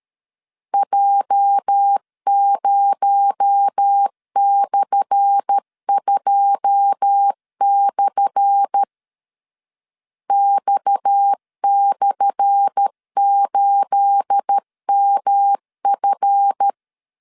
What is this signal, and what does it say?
Morse code, heard as a keyed tone, about 13 wpm, 800 Hz. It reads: J0/2/ X/8MF
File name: LEZIONI / 10 SIGNS / 10A SLASH 12wpm 2314